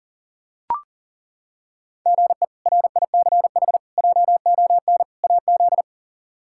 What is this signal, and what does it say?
Morse code, heard as a keyed tone, about 38 wpm, 700 Hz. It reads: GE RICH JON AZ
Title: K1USN SST QSO Practice - 40wpm